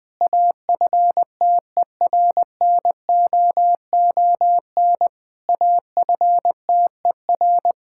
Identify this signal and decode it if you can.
Morse code, heard as a keyed tone, about 20 wpm, 700 Hz. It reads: AFTERNOON AFTER
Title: Difficult Sentences with Hints - 20wpm